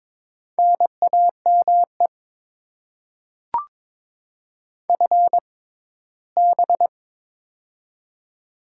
Morse code, heard as a keyed tone, about 22 wpm, spaced as for 15 wpm, 690 Hz. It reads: NAME FB
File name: Top 20 QSO Elements - 6x Character Spacing - 22wpm